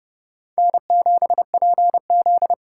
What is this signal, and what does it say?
Morse code, heard as a keyed tone, about 37 wpm, 700 Hz. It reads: N7PZ
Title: Speed Racing - USA 1x2 Call Signs 30wpm